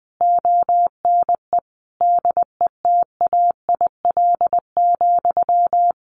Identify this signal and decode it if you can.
Morse code, heard as a keyed tone, about 20 wpm, 700 Hz. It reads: ONE DETAIL,
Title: Sentences from Top 1500 Words 20wpm